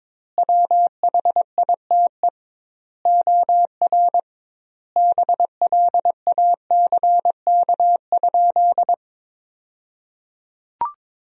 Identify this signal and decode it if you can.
Morse code, heard as a keyed tone, about 22 wpm, 700 Hz. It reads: WHITE OR BLACK?
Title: 2x Word-Spacing - Sentences from Top 400 Words - 22wpm